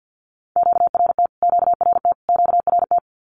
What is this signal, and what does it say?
Morse code, heard as a keyed tone, about 50 wpm, 700 Hz. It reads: QRT QRT QRT
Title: Q-Codes Repeated 3x 50wpm